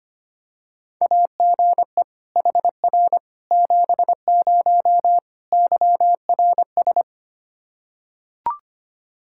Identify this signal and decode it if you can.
Morse code, heard as a keyed tone, about 25 wpm, 700 Hz. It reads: AGE HR 70 YRS
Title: Practice QSO - Exchange 3 - More Info - 25wpm